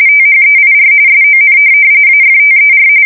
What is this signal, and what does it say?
RTTY, about 45.45 baud. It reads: THEIR SON AFTER
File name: fsk170 / USS-New-Jersey